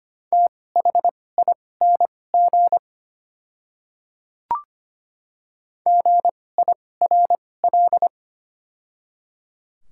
Morse code, heard as a keyed tone, about 25 wpm, spaced as for 18 wpm, 700 Hz. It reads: THING GIRL
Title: 2x Character Spacing - Top 500 Words - 25wpm